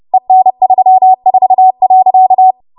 Morse code, heard as a keyed tone, about 30 wpm, 750 Hz. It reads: EN34.
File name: Morse / 30WPM